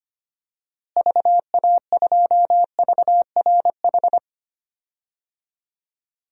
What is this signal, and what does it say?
Morse code, heard as a keyed tone, about 25 wpm, 700 Hz. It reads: VA2VRH
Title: DXCC-100 Call Signs - 6 Characters - Repeated 3x - 25wpm